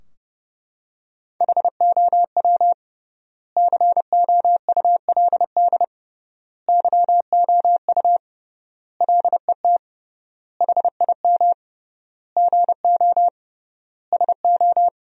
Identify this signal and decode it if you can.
Morse code, heard as a keyed tone, about 29 wpm, 700 Hz. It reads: HOW COULD YOU LET HIM GO SO
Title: 3x Word-Spacing - Sentences from Top 300 Words - 30wpm